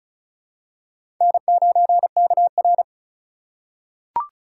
Morse code, heard as a keyed tone, about 34 wpm, 700 Hz. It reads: N9KR
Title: K1USN SST QSO Practice - 35wpm